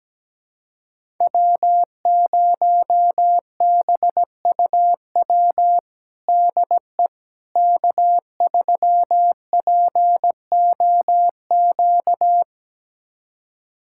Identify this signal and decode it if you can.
Morse code, heard as a keyed tone, about 17 wpm, 700 Hz. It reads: W0BUW DE K3POQ
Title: USA Callsign DE Callsign - 5 Characters - 17wpm